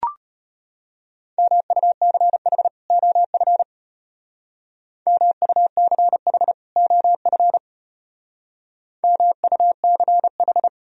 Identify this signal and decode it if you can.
Morse code, heard as a keyed tone, about 32 wpm, 700 Hz. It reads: MUCH OF MUCH OF MUCH